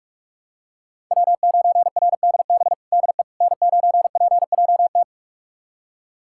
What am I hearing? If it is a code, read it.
Morse code, heard as a keyed tone, about 46 wpm, 700 Hz. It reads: W9RDB DE N9PJT
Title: USA Callsign DE Callsign - 5 Characters - 45wpm